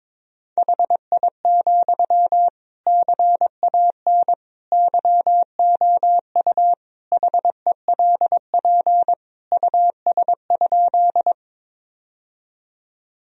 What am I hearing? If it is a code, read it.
Morse code, heard as a keyed tone, about 22 wpm, 700 Hz. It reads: HI, CAN YOU HELP US?
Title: Sentences from Top 1000 Words - Review Part 3 - 22wpm